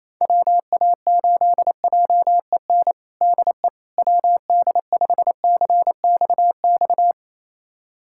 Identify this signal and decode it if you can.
Morse code, heard as a keyed tone, about 28 wpm, 700 Hz. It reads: WA8JEN DE WD5CXX
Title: USA Callsign DE Callsign 28wpm